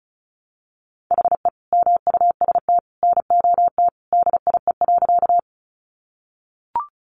Morse code, heard as a keyed tone, about 34 wpm, 700 Hz. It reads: HE MUST NOT DIE.
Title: Sentences from Top 1300 Words 35wpm